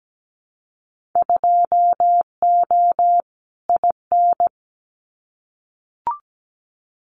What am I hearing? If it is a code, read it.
Morse code, heard as a keyed tone, about 17 wpm, 700 Hz. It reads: 2O IN
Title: Winter Field Day Exchanges 17wpm